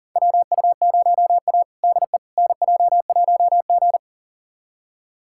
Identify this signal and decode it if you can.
Morse code, heard as a keyed tone, about 40 wpm, 700 Hz. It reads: WU0A DE NJ1G
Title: USA Callsign DE Callsign - 4 Characters - 40wpm